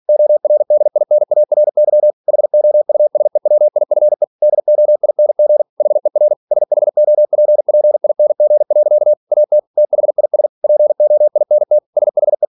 Morse code, heard as a keyed tone, about 46 wpm, 600 Hz. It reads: ORDINARY HOUSEWIFE DOING HER SHOPPING. AT THIS POINT SHE
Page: Never 47wpm